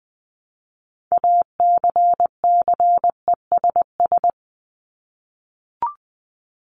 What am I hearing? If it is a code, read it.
Morse code, heard as a keyed tone, about 20 wpm, 700 Hz. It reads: ACCESS
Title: Top 501-750 Words 20wpm